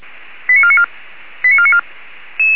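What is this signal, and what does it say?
Five-tone selective call (ZVEI), heard as quick runs of five tones, tones of about 70 ms. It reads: 89494 89494
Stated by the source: Received on an SDR by feuerwehr-orlamuende.org (fire brigade Orlamünde).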